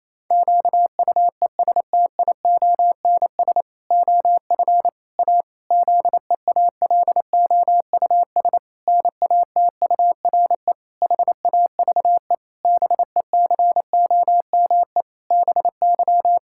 Morse code, heard as a keyed tone, about 28 wpm, 700 Hz. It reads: QUESTIONS OF A ZEALOUS NATURE HAVE BECOME BY
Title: Pangram Challenge v2 - 28wpm